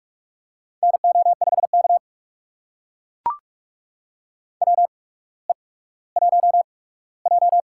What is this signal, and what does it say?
Morse code, heard as a keyed tone, about 44 wpm, spaced as for 19 wpm, 690 Hz. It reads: NO5K WE1J
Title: USA Call Signs - 4 Characters - 8x Character Spacing - 45wpm